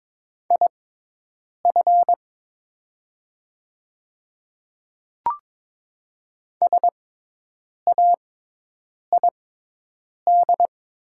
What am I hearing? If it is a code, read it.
Morse code, heard as a keyed tone, about 22 wpm, spaced as for 8 wpm, 700 Hz. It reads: IF SAID
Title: 6x Character Spacing - Top 100 Words - 22wpm